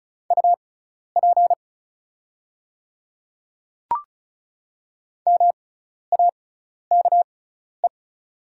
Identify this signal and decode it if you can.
Morse code, heard as a keyed tone, about 36 wpm, spaced as for 12 wpm, 690 Hz. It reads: UP MAKE
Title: 6x Character Spacing - Top 200 Words - 35wpm